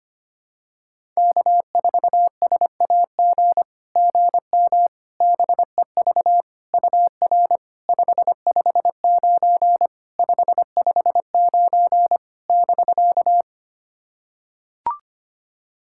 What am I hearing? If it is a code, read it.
Morse code, heard as a keyed tone, about 25 wpm, 700 Hz. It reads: K4SAG GM BEV UR 559 559 <BK>
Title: SOTA QSO Practice 25wpm